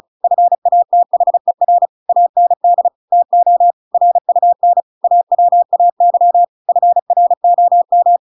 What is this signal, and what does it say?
Morse code, heard as a keyed tone, about 35 wpm, 700 Hz. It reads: FATHER AND TO RUN AWAY FROM